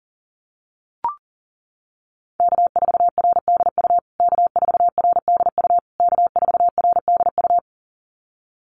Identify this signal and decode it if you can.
Morse code, heard as a keyed tone, about 40 wpm, 700 Hz. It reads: K4RDU K4RDU K4RDU
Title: USA Call Signs Repeated 3x 40wpm